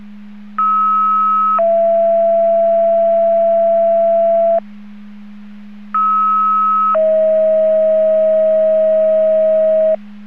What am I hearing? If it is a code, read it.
Two-tone page, heard as a one-second tone, then a three-second tone, about 1290 then 670 Hz, about 1290 then 640 Hz. It is heard from 0.6 to 9.9 s.